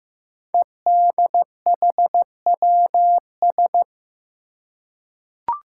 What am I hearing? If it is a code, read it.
Morse code, heard as a keyed tone, about 15 wpm, 700 Hz. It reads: EDHWS
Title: Five Challenging Letter-Number Combinations 15wpm